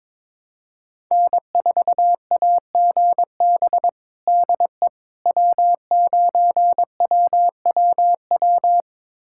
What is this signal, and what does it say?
Morse code, heard as a keyed tone, about 22 wpm, 700 Hz. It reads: N4AGB DE W9WWW
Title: USA Callsign DE Callsign - 5 Characters - 22wpm